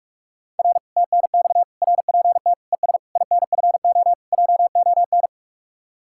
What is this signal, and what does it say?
Morse code, heard as a keyed tone, about 48 wpm, 700 Hz. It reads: R TNX RPT ES INFO JON